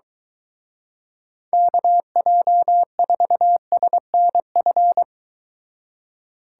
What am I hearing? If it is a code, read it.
Morse code, heard as a keyed tone, about 23 wpm, 700 Hz. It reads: KJ4SNF